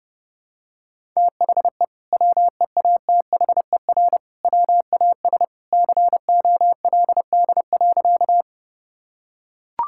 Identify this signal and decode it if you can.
Morse code, heard as a keyed tone, about 32 wpm, 700 Hz. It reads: THE WEATHER WAS COLD.
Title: Sentences from Top 1000 Words - Review Part 5 - 30wpm